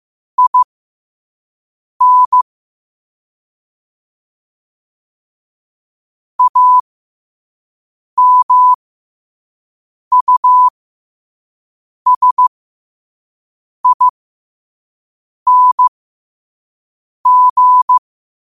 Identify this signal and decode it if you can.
Morse code, heard as a keyed tone, about 15 wpm, spaced as for 5 wpm, 1000 Hz. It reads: IN AMUSING
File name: qotdfiles / 5wpm